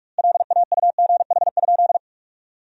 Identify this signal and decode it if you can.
Morse code, heard as a keyed tone, about 42 wpm, 700 Hz. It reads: LAUGH?
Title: 2x Word-Spacing - Sentences from Top 500 Words - 45wpm